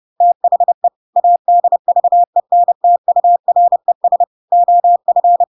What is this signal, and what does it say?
Morse code, heard as a keyed tone, about 30 wpm, 700 Hz. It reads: THE ADVENTURES OF
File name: Pinocchio / Pinocchio ch7 30 30 ik1hge